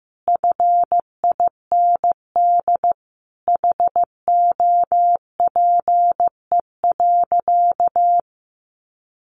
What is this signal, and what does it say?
Morse code, heard as a keyed tone, about 15 wpm, 700 Hz. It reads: FIND HOPE.
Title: Sentences from Top 1300 Words 15wpm